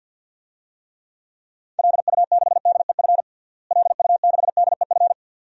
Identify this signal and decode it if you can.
Morse code, heard as a keyed tone, about 46 wpm, 700 Hz. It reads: LU6DEF LU6DEF
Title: DXCC-100 Call Signs - 6 Characters - Repeated 2x - 50wpm